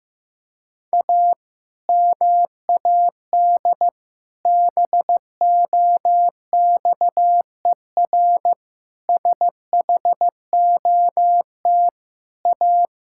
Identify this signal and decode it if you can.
Morse code, heard as a keyed tone, about 15 wpm, 700 Hz. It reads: A MAD BOXER SHOT A
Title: Pangram Challenge v2 - 15wpm